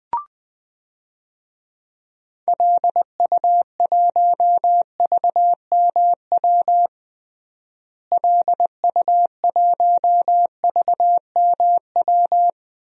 Morse code, heard as a keyed tone, about 20 wpm, 700 Hz. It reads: LU1VMW LU1VMW
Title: DXCC-100 Call Signs - 6 Characters - Repeated 3x - 20wpm